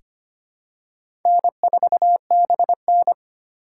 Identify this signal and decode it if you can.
Morse code, heard as a keyed tone, about 25 wpm, 700 Hz. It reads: N4BN
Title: USA Call Signs - 4 Characters - 3x Character Spacing - 25wpm